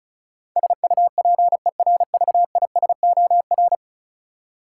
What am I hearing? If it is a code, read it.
Morse code, heard as a keyed tone, about 34 wpm, 700 Hz. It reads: SUPERVISOR
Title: Practice QSO - Exchange 3 - More Info - 35wpm